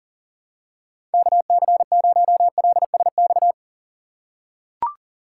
Morse code, heard as a keyed tone, about 40 wpm, 700 Hz. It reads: KC0LSX
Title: USA Call Signs - 6 Characters - 4x Character Spacing - 40wpm